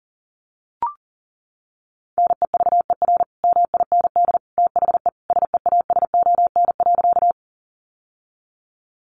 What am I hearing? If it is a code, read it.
Morse code, heard as a keyed tone, about 40 wpm, 700 Hz. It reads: NEVER MIND THE SEASON.
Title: Sentences from Top 1400 Words 40wpm